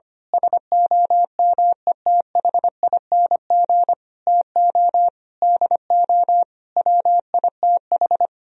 Morse code, heard as a keyed tone, about 24 wpm, 700 Hz. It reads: SOMETHING TO DO WITH
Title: Speed-Racing - Sets of 5 Words 25wpm